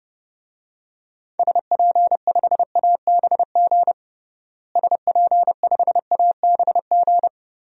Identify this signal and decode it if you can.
Morse code, heard as a keyed tone, about 30 wpm, 700 Hz. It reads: SP5ABG SP5ABG
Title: DXCC-100 Call Signs - 6 Characters - Repeated 2x - 30wpm